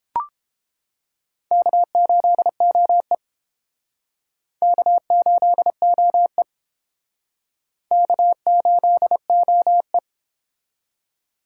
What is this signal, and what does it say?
Morse code, heard as a keyed tone, about 29 wpm, 700 Hz. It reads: K8OE K8OE K8OE